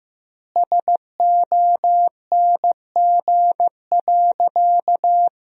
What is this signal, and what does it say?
Morse code, heard as a keyed tone, about 15 wpm, 700 Hz. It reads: SONG.
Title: 2x Word-Spacing - Sentences from Top 400 Words - 15wpm